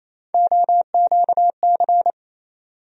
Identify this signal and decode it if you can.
Morse code, heard as a keyed tone, about 28 wpm, 700 Hz. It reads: OQC
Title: USA Call Sign Postfixes - 3 Letters 28wpm